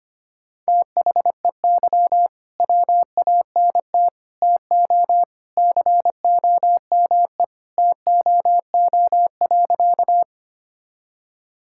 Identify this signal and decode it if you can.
Morse code, heard as a keyed tone, about 25 wpm, 700 Hz. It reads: THEY WANT TO COME TOO.
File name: practice / 25wpm / Sentences with 3-Word Phrases - 25wpm